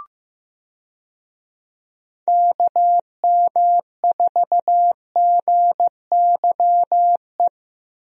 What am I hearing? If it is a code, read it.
Morse code, heard as a keyed tone, about 15 wpm, 700 Hz. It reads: KM4GYE
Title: USA Call Signs - 6 Characters - Repeated 3x - 15wpm